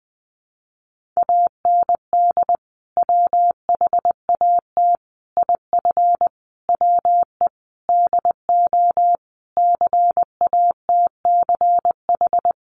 Morse code, heard as a keyed tone, about 20 wpm, 700 Hz. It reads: AND WHAT IF WE DO CATCH
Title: Sentences from Top 1500 Words 20wpm